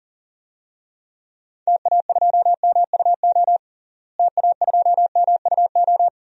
Morse code, heard as a keyed tone, about 40 wpm, 700 Hz. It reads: TA2MUO TA2MUO
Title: DXCC-100 Call Signs - 6 Characters - Repeated 3x - 40wpm